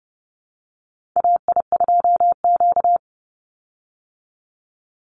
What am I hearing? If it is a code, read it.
Morse code, heard as a keyed tone, about 30 wpm, 700 Hz. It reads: AI2Q
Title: Call Signs Repeated 2x - 30wpm